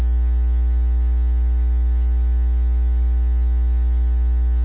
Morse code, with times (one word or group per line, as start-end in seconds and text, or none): none